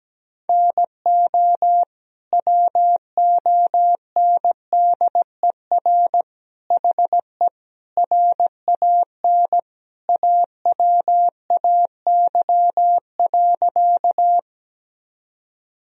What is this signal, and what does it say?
Morse code, heard as a keyed tone, about 17 wpm, 700 Hz. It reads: NO WONDER HE RAN AWAY.
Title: Sentences from Top 1000 Words - Review Part 3 - 17wpm